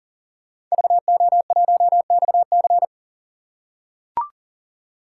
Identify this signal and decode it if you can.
Morse code, heard as a keyed tone, about 40 wpm, 700 Hz. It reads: VO1XC
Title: K1USN SST QSO Practice - 40wpm